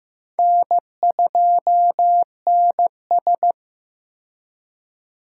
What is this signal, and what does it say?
Morse code, heard as a keyed tone, about 16 wpm, 700 Hz. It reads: N2NS
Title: K1USN SST QSO Practice - 15wpm